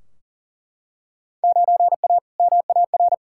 Morse code, heard as a keyed tone, about 40 wpm, 700 Hz. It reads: 9A MAR
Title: Field Day Exchanges Encore - 40wpm